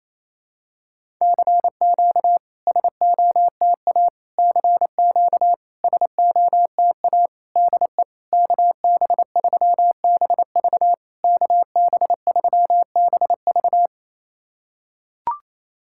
Morse code, heard as a keyed tone, about 28 wpm, 700 Hz. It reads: CQ SOTA CQ SOTA DE KB3BV KB3BV
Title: SOTA QSO Practice 28wpm